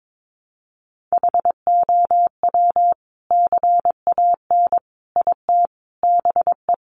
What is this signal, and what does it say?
Morse code, heard as a keyed tone, about 22 wpm, 700 Hz. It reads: HOW CAN IT BE
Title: Sentences from Top 1400 Words 22wpm